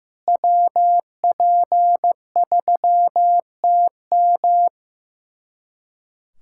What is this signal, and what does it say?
Morse code, heard as a keyed tone, about 15 wpm, 700 Hz. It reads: WP3TM
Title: USA Callsign DE Callsign - 5 Characters - 15wpm